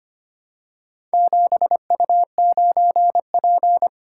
Morse code, heard as a keyed tone, about 25 wpm, 700 Hz. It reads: 7U9P
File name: practice / 25wpm / Four Letter-Number Combinations - 25wpm